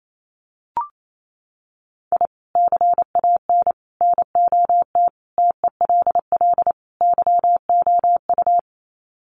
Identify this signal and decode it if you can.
Morse code, heard as a keyed tone, about 28 wpm, 700 Hz. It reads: I CAN NOT TELL YOU
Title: Sets of 5 Words 28wpm